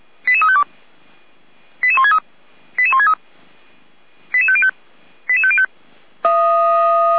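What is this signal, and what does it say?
Five-tone selective call (ZVEI), heard as quick runs of five tones, tones of about 80 ms. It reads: 88352 88163 88163 88505 88505
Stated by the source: ZVEI five-tone selective call received on SDR.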